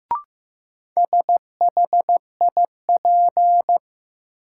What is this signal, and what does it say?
Morse code, heard as a keyed tone, about 15 wpm, 700 Hz. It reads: SHIP
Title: Sentences from Top 400 Words - New Word Drill - 15wpm